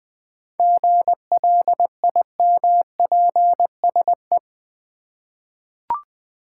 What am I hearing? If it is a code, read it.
Morse code, heard as a keyed tone, about 20 wpm, 700 Hz. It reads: GLIMPSE
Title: 7 Letter Words - First 3 Letters Given - 20wpm